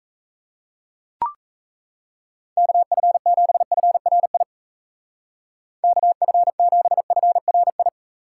Morse code, heard as a keyed tone, about 37 wpm, 700 Hz. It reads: KF7FRI KF7FRI